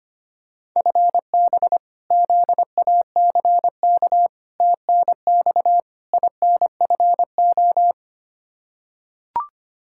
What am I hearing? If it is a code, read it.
Morse code, heard as a keyed tone, about 25 wpm, 700 Hz. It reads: FB ZACK TNX INFO
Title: Practice QSO - Exchange 3 - More Info - 25wpm